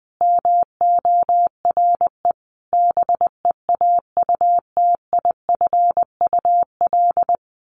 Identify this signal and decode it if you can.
Morse code, heard as a keyed tone, about 20 wpm, 700 Hz. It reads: MORE BEAUTIFUL
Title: Sentences from Top 1200 Words 20wpm